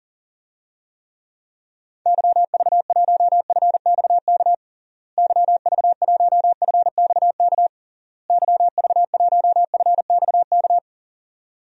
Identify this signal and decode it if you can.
Morse code, heard as a keyed tone, about 40 wpm, 700 Hz. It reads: YV1FXK YV1FXK YV1FXK
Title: DXCC-100 Call Signs - 6 Characters - Repeated 3x - 40wpm